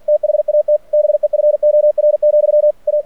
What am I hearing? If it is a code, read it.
Morse code, heard as a keyed tone, about 50 wpm, 600 Hz. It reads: THAT BEFOR, A
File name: archived / CRSnet